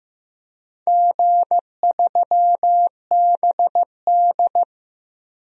Morse code, heard as a keyed tone, about 15 wpm, 700 Hz. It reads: G3BD
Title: DXCC-100 Call Signs - 4 Characters - Repeated 2x - 15wpm